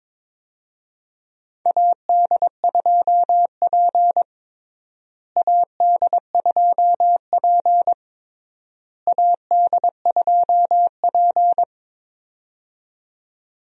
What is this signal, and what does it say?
Morse code, heard as a keyed tone, about 22 wpm, 700 Hz. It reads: AD2P AD2P AD2P
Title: USA Call Signs - 4 Characters - Repeated 3x - 22wpm